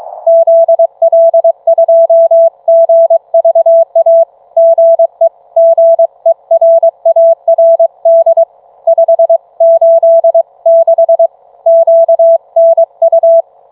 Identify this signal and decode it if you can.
Morse code, heard as a keyed tone, about 23 wpm, 660 Hz. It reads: ZL2GVA GE GERARD 586 QNU